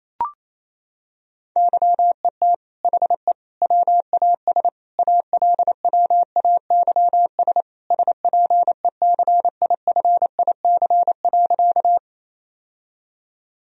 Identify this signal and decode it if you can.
Morse code, heard as a keyed tone, about 28 wpm, 700 Hz. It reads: YET HE WAS ALWAYS SPECIFIC.
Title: Sentences from Top 1000 Words - Review Part 3 - 28wpm